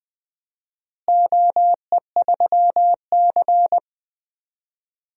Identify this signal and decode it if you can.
Morse code, heard as a keyed tone, about 20 wpm, 700 Hz. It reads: OE3C
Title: DXCC-100 Call Signs - 4 Characters - Repeated 3x - 20wpm